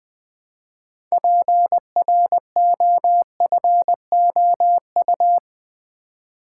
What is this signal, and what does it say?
Morse code, heard as a keyed tone, about 20 wpm, 700 Hz. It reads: PROFOU
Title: Build A Word - 2 Letters At A Time - 8-Letter Words - 20wpm